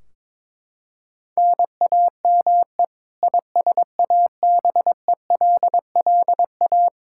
Morse code, heard as a keyed tone, about 22 wpm, 700 Hz. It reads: NAME ISABELLA
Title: K1USN SST QSO Practice 22wpm